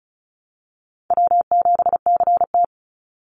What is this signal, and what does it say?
Morse code, heard as a keyed tone, about 34 wpm, 700 Hz. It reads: W7CT
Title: Easy Call Signs Repeated 2x - 35wpm